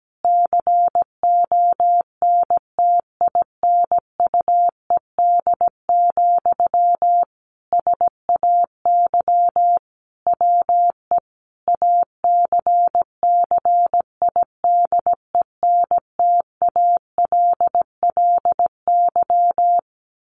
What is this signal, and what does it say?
Morse code, heard as a keyed tone, about 17 wpm, 700 Hz. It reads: CONTINUED, SAY WE ACCIDENTALLY
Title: A Sound Of Thunder 17wpm